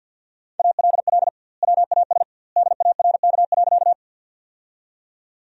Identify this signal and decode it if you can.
Morse code, heard as a keyed tone, about 46 wpm, 700 Hz. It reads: ALL WAS DARK.
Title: 2x Word-Spacing - Sentences from Top 500 Words - 50wpm